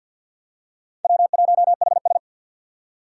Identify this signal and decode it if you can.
Morse code, heard as a keyed tone, about 47 wpm, 700 Hz. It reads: W1HS
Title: USA Call Signs - 3x Character Spacing - 50wpm